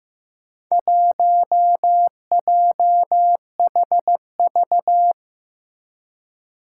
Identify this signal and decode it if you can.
Morse code, heard as a keyed tone, about 15 wpm, 700 Hz. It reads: 1JHV
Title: Four Challenging Letter-Number Combinations 15wpm